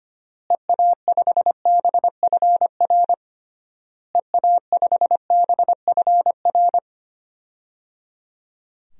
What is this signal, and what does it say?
Morse code, heard as a keyed tone, about 25 wpm, 700 Hz. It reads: EA5BFR EA5BFR
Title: DXCC-100 Call Signs - 6 Characters - Repeated 3x - 25wpm